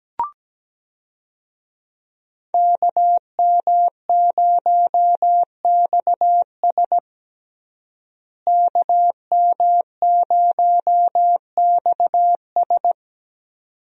Morse code, heard as a keyed tone, about 17 wpm, 700 Hz. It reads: KM0XS KM0XS
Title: USA Call Signs - 5 Characters - Repeated 2x - 17wpm